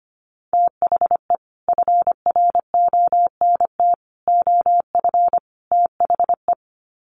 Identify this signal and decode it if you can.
Morse code, heard as a keyed tone, about 25 wpm, 700 Hz. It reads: THE FRONT OF THE
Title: Sets of 4 Words 25wpm